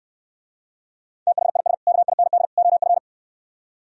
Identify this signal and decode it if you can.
Morse code, heard as a keyed tone, about 67 wpm, 700 Hz. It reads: THIS KIND OF